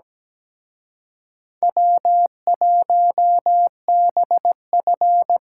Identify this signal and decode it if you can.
Morse code, heard as a keyed tone, about 20 wpm, 700 Hz. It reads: W1BF
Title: Speed Racing - USA 1x2 Call Signs 17wpm